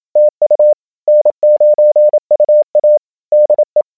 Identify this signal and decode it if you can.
Morse code, heard as a keyed tone, about 27 wpm, 590 Hz. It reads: TU N9UA DE
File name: a92ge / SDRQuickAudio